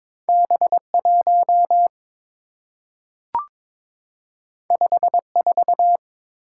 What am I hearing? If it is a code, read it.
Morse code, heard as a keyed tone, about 22 wpm, 700 Hz. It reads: B1 54
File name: practice / 22wpm / Two Letter-Number Combinations - 22wpm